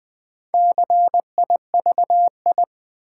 Morse code, heard as a keyed tone, about 20 wpm, 700 Hz. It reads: CIVI
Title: Build A Word - 4 Letters At A Time - 12-Letter Words - 20wpm